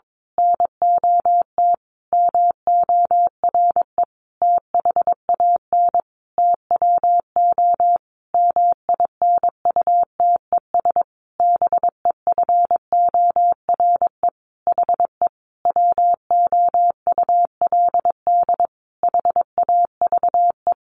Morse code, heard as a keyed tone, about 22 wpm, 700 Hz. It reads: NOT MORE THAN TWO MINUTES BEFORE HE WOULD HAVE
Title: A Sound Of Thunder 22wpm